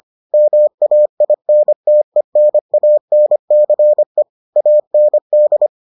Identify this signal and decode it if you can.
Morse code, heard as a keyed tone, about 25 wpm, 600 Hz. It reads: MAINTENANCE AND